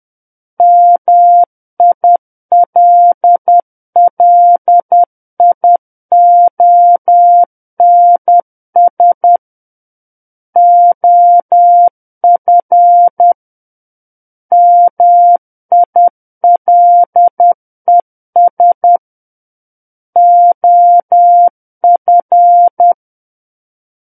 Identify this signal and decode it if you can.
Morse code, heard as a keyed tone, about 10 wpm, 700 Hz. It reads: MILLIONS OF MILES OF
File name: Book One / 01-Chapter 1 - 10 WPM